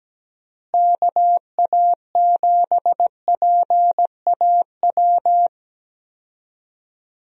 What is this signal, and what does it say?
Morse code, heard as a keyed tone, about 17 wpm, 700 Hz. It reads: KA7PAW
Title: USA Call Signs - 6 Characters - Repeated 3x - 17wpm